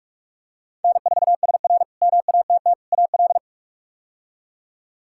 Morse code, heard as a keyed tone, about 48 wpm, 700 Hz. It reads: N4SR MATT AL
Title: K1USN SST QSO Practice - 45wpm